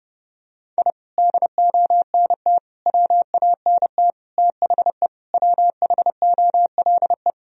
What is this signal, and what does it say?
Morse code, heard as a keyed tone, about 32 wpm, 700 Hz. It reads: I DONT WANT THE WHOLE
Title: Sentences from Top 1000 Words - Review Part 2 - 30wpm